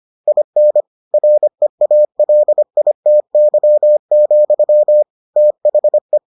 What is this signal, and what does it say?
Morse code, heard as a keyed tone, about 25 wpm, 600 Hz. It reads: IN REALITY, THE